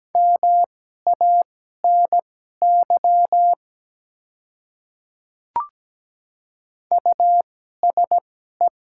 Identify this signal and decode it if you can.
Morse code, heard as a keyed tone, about 17 wpm, spaced as for 12 wpm, 700 Hz. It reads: MANY USE
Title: 2x Character Spacing - Top 100 Words - 17wpm